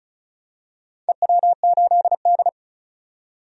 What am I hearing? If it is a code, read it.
Morse code, heard as a keyed tone, about 36 wpm, 700 Hz. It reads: EW8D
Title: DXCC-100 Call Signs - 4 Characters - 35wpm